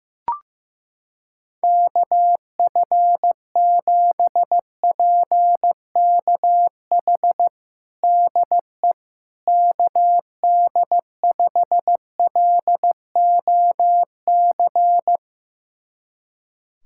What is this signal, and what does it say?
Morse code, heard as a keyed tone, about 15 wpm, 700 Hz. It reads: KF7PKH DE KD5LOC
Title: USA Callsign DE Callsign 15wpm